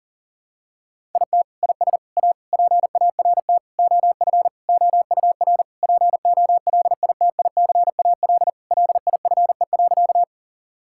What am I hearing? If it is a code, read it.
Morse code, heard as a keyed tone, about 37 wpm, 700 Hz. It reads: IT IS A PART OF OUR POLITICAL LIFE.
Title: Sentences from Top 1000 Words - Review Part 5 - 40wpm